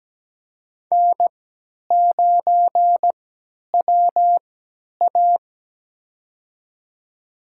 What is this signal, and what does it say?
Morse code, heard as a keyed tone, about 17 wpm, spaced as for 10 wpm, 700 Hz. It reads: N9WA
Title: USA Call Signs - 4 Characters - 3x Character Spacing - 17wpm